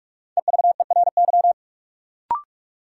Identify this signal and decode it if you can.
Morse code, heard as a keyed tone, about 42 wpm, 700 Hz. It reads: EVERY
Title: Sets of 5 Words - New Word Drill - 45wpm